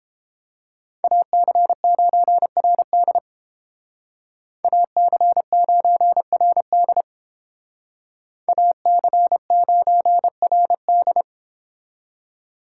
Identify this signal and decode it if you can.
Morse code, heard as a keyed tone, about 30 wpm, 700 Hz. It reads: AC9RD AC9RD AC9RD